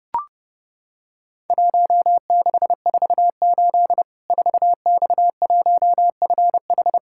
Morse code, heard as a keyed tone, about 30 wpm, 700 Hz. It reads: 1648 4X1FH
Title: Worked All Europe DX Contest - QTC Exchange - 30wpm